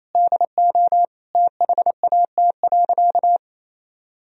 Morse code, heard as a keyed tone, about 28 wpm, 700 Hz. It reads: DO THAT.
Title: Sentences from Top 1000 Words - Review Part 4 - 28wpm